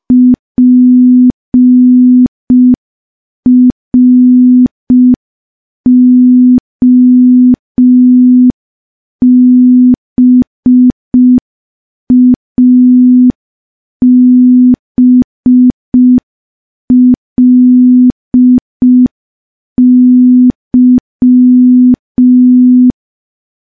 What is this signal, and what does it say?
Morse code, heard as a keyed tone, about 5 wpm, 250 Hz. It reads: PROBABLY
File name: morse / morse-training-05WPM